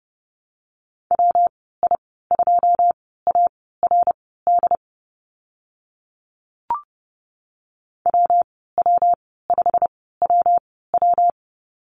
Morse code, heard as a keyed tone, about 30 wpm, spaced as for 17 wpm, 700 Hz. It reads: WI2ARD WW5WW
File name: Lesson 19g - Letter w - Call Signs 30wpm